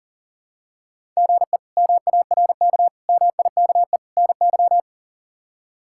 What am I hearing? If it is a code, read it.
Morse code, heard as a keyed tone, about 40 wpm, 700 Hz. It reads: GE MARK MIKE NY
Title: K1USN SST QSO Practice - 40wpm